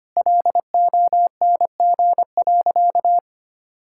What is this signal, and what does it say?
Morse code, heard as a keyed tone, about 25 wpm, 700 Hz. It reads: LONG.
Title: Sentences from Top 1600 Words 25wpm